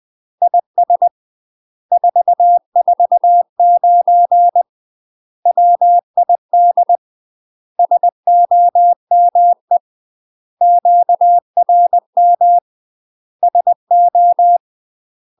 Morse code, heard as a keyed tone, about 20 wpm, 700 Hz. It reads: IS 449 WID SOME QRM SO
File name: UK MASTER QSOs TRAINING / 20WPM / UK MASTER QSOs 0011